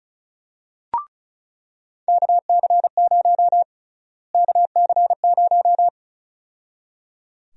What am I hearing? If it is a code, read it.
Morse code, heard as a keyed tone, about 34 wpm, 700 Hz. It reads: KC0 KC0
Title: USA Call Sign Prefixes - Repeated 2x - 35wpm